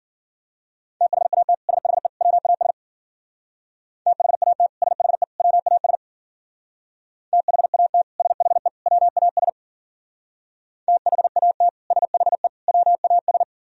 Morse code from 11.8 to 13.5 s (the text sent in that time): SHE WAS